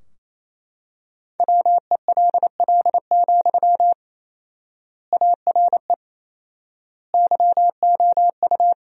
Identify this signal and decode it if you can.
Morse code, heard as a keyed tone, about 28 wpm, 700 Hz. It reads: WELL, ARE YOU
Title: 4x Word-Spacing - Sentences from Top 400 Words - 28wpm